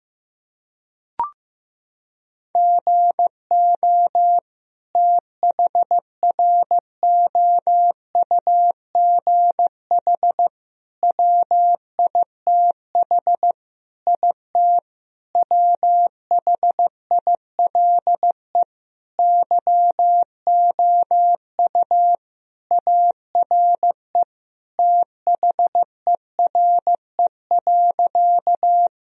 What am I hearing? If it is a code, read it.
Morse code, heard as a keyed tone, about 16 wpm, 700 Hz. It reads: GO THROUGH WITH IT WHILE YOU ARE THERE.
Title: Sentences from Top 1000 Words - Review Part 1 - 15wpm